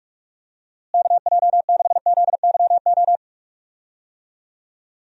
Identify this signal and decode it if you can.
Morse code, heard as a keyed tone, about 41 wpm, 700 Hz. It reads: KJ6ZYO
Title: USA Call Signs - 6 Characters - Repeated 3x - 45wpm